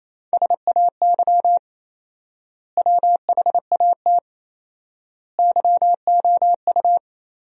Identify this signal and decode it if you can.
Morse code, heard as a keyed tone, about 28 wpm, 700 Hz. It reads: SAY WHAT YOU
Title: 4x Word-Spacing - Sentences from Top 200 Words - 28wpm